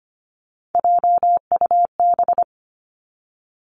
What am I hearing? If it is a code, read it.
Morse code, heard as a keyed tone, about 25 wpm, 700 Hz. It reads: JUB
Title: Three Challenging Letter-Number Combinations 25wpm